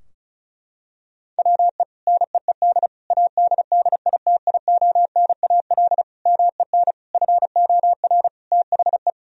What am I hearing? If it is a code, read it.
Morse code, heard as a keyed tone, about 36 wpm, 700 Hz. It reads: WE NEED ADDITIONAL MEN FOR THE
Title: Sentences from Top 1000 Words - Review Part 4 - 35wpm